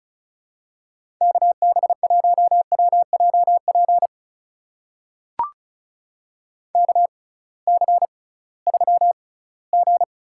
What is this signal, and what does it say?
Morse code, heard as a keyed tone, about 36 wpm, spaced as for 21 wpm, 700 Hz. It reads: KB1WJP KC3G
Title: USA Call Signs - 6 Characters - 6x Character Spacing - 35wpm